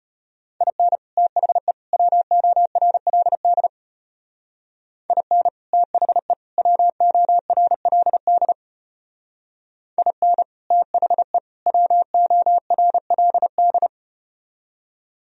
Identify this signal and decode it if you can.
Morse code, heard as a keyed tone, about 34 wpm, 700 Hz. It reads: IN THE WORLD IN THE WORLD IN THE WORLD